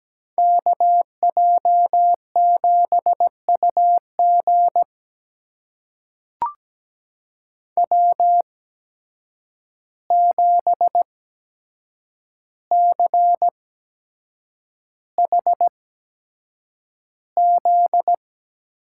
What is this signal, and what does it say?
Morse code, heard as a keyed tone, about 17 wpm, spaced as for 7 wpm, 700 Hz. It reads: KJ7UG W7CHZ
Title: USA Call Signs - 5 Characters - 8x Character Spacing - 17wpm